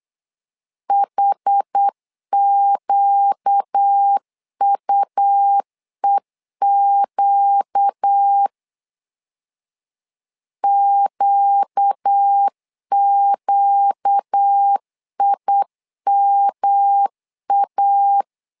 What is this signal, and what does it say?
Morse code, heard as a keyed tone, about 8 wpm, 800 Hz. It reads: HQUEQ QQIMA
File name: 4 Q 8wpm